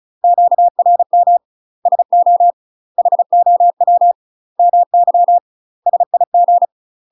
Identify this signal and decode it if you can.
Morse code, heard as a keyed tone, about 36 wpm, 700 Hz. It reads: QRM SO HOW MY SIG
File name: UK MASTER QSOs TRAINING / 35WPM / UK MASTER QSOs 0017